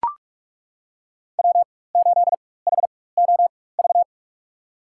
Morse code, heard as a keyed tone, about 41 wpm, spaced as for 20 wpm, 700 Hz. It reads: W8HOV
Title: USA Call Signs - 5 Characters - 4x Character Spacing - 45wpm